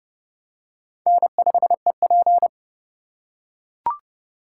Morse code, heard as a keyed tone, about 30 wpm, 700 Hz. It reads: N5EP
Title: USA Call Signs - 4 Characters - 2x Character Spacing - 30wpm